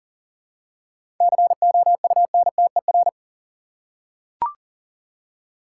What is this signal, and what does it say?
Morse code, heard as a keyed tone, about 40 wpm, 700 Hz. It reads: COUNTER
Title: Seven Letter Words - 40wpm